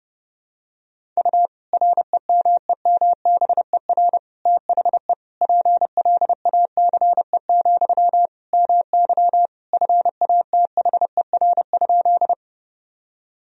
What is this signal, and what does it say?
Morse code, heard as a keyed tone, about 30 wpm, 700 Hz. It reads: U REMEMBER THE PLACE, MY FATHER?
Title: Sentences from Top 400 Words and Abbreviations 30wpm